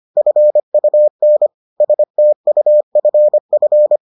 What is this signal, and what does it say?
Morse code, heard as a keyed tone, about 25 wpm, 600 Hz. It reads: FUN STUFF